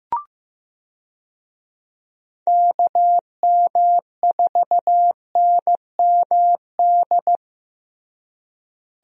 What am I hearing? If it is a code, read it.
Morse code, heard as a keyed tone, about 16 wpm, 700 Hz. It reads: KM4NMD
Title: USA Call Signs - 6 Characters - Repeated 2x - 15wpm